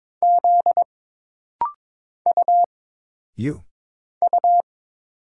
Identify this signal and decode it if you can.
Morse code, heard as a keyed tone, about 22 wpm, 700 Hz. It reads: Z U U
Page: This rapid-fire timing is geared to push your ICR (Instant Character Recognition) skills.